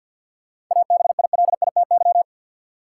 Morse code, heard as a keyed tone, about 52 wpm, 700 Hz. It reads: ABILITY
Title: Sentences from Top 1000 Words - Review Part 3 - New Word Drill - 50wpm